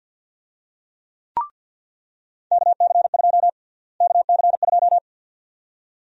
Morse code, heard as a keyed tone, about 46 wpm, 700 Hz. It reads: KC2 KC2
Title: USA Call Sign Prefixes - Repeated 2x - 50wpm